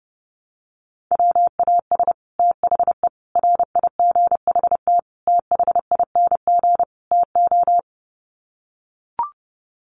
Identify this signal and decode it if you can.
Morse code, heard as a keyed tone, about 30 wpm, 700 Hz. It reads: WAS THE RIGHT THING TO
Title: Sets of 5 Words Encore2 30wpm